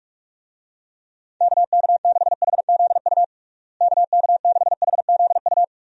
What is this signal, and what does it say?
Morse code, heard as a keyed tone, about 41 wpm, 700 Hz. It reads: KK6HZU KK6HZU
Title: USA Call Signs - 6 Characters - Repeated 3x - 45wpm